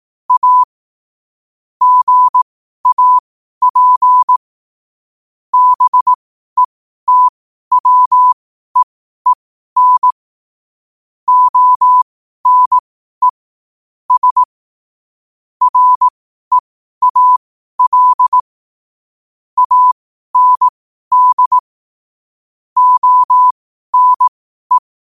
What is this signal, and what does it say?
Morse code, heard as a keyed tone, about 18 wpm, spaced as for 12 wpm, 1000 Hz. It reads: A GAP BETWEEN ONE S REAL AND ONE
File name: qotdfiles / 12wpm